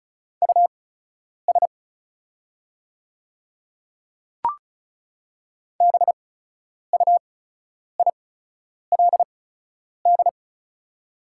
Morse code, heard as a keyed tone, about 34 wpm, spaced as for 10 wpm, 700 Hz. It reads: US BUILD
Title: 8x Character Spacing - Top 200 Words - 35wpm